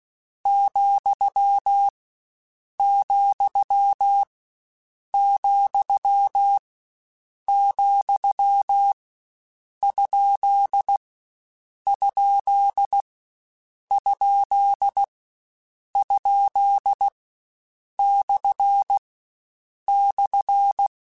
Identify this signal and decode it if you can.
Morse code, heard as a keyed tone, about 16 wpm, 800 Hz. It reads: , , , , ? ? ? ? / /
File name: Misc / Misc-mp3s / Punct @15